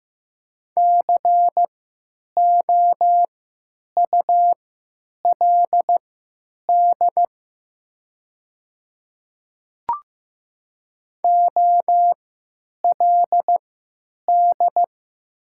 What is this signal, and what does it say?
Morse code, heard as a keyed tone, about 15 wpm, spaced as for 9 wpm, 700 Hz. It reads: COULD OLD
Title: 3x Character Spacing - Top 200 Words - 15wpm